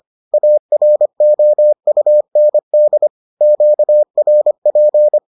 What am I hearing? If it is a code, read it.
Morse code, heard as a keyed tone, about 25 wpm, 600 Hz. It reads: AROUND QRP